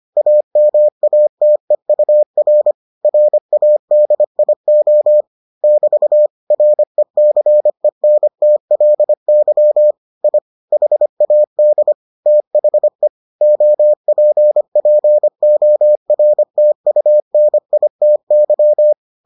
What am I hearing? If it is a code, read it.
Morse code, heard as a keyed tone, about 25 wpm, 600 Hz. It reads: AMATEUR RADIO <BT> RECENTLY I HAD THE OPPORTUNITY